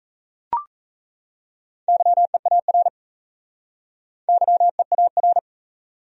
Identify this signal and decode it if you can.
Morse code, heard as a keyed tone, about 36 wpm, 700 Hz. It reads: YEAR YEAR